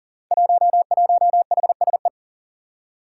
Morse code, heard as a keyed tone, about 38 wpm, 700 Hz. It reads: 11HSE
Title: Five Challenging Letter-Number Combinations 40wpm